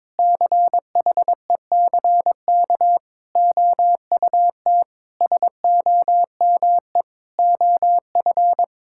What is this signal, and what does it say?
Morse code, heard as a keyed tone, about 21 wpm, 700 Hz. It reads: CHECK OUT SOME OF
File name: Sets of 9 Words - 22wpm